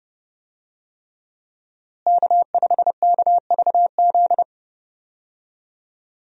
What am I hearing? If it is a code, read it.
Morse code, heard as a keyed tone, about 30 wpm, 700 Hz. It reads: K5KVZ
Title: USA Call Signs - 5 Characters - Repeated 3x - 30wpm